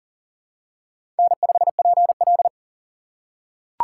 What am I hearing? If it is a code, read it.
Morse code, heard as a keyed tone, about 41 wpm, 700 Hz. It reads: N5PL
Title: USA Call Signs - 4 Characters - 4x Character Spacing - 40wpm